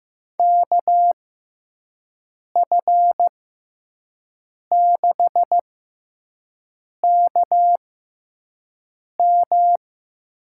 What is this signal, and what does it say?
Morse code, heard as a keyed tone, about 15 wpm, spaced as for 5 wpm, 700 Hz. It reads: KF6KM
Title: USA Call Signs - 6x Character Spacing - 15wpm